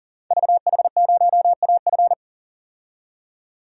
Morse code, heard as a keyed tone, about 39 wpm, 700 Hz. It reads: VH0AF
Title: Code Groups - 5 Letter:Numbers - 40wpm